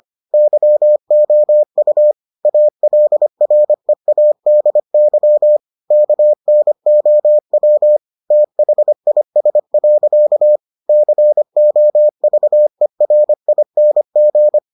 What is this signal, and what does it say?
Morse code, heard as a keyed tone, about 25 wpm, 600 Hz. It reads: YOU ALREADY KNOW THIS. COVERING